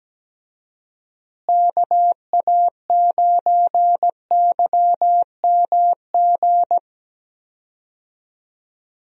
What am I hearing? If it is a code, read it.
Morse code, heard as a keyed tone, about 17 wpm, 700 Hz. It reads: KA9YMG
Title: USA Call Signs - 6 Characters - Repeated 3x - 17wpm